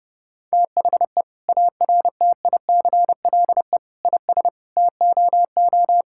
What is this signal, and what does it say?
Morse code, heard as a keyed tone, about 32 wpm, 700 Hz. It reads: THE ARTICLE IS TOO
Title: Sentences from Top 1000 Words - Review Part 5 - 30wpm